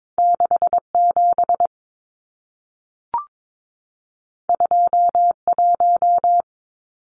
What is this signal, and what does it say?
Morse code, heard as a keyed tone, about 22 wpm, 700 Hz. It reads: 67 21
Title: Numbers - 2-Digit 22wpm